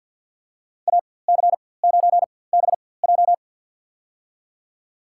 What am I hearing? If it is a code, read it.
Morse code, heard as a keyed tone, about 49 wpm, spaced as for 23 wpm, 700 Hz. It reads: AC9BJ
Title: USA Call Signs - 5 Characters - 4x Character Spacing - 50wpm